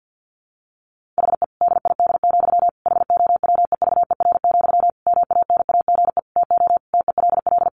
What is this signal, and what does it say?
Morse code, heard as a keyed tone, about 50 wpm, 700 Hz. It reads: HE DID, HOWEVER, MANAGE TO TELL
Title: Sentences from Top 1400 Words 50wpm